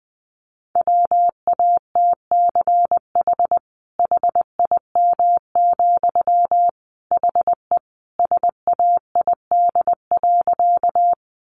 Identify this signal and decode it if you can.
Morse code, heard as a keyed tone, about 19 wpm, 700 Hz. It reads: WATCH HIM, HE SAID.
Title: Sentences from Top 300 Words 20wpm